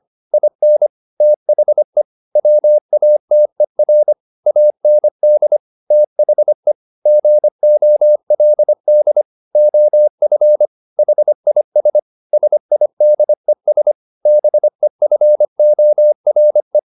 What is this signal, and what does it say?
Morse code, heard as a keyed tone, about 25 wpm, 600 Hz. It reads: IN THE WATER AND THE GOLD OF HIS SIDES BEFORE
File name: the old man and the sea part 1 25wpm